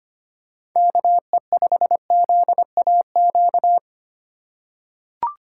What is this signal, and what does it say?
Morse code, heard as a keyed tone, about 25 wpm, 700 Hz. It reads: KE5ZAQ
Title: USA Call Signs - 6 Characters - 2x Character Spacing - 25wpm